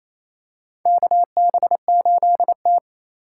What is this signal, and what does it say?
Morse code, heard as a keyed tone, about 28 wpm, 700 Hz. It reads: KB8T
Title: USA Call Signs - 4 Characters - 2x Character Spacing - 28wpm